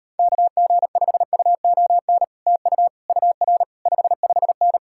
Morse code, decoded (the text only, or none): KG5UON TU UR 55N